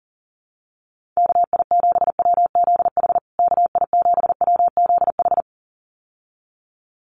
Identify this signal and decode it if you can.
Morse code, heard as a keyed tone, about 37 wpm, 700 Hz. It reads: KI7WZH KI7WZH
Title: USA Call Signs Repeated 2x 40wpm